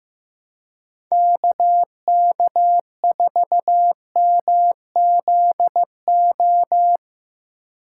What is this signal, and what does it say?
Morse code, heard as a keyed tone, about 15 wpm, 700 Hz. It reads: KK4MZO
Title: USA Call Signs - 6 Characters - 2x Character Spacing - 15wpm